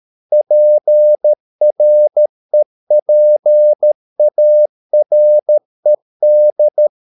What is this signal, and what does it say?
Morse code, heard as a keyed tone, about 13 wpm, 600 Hz. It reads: PREPARED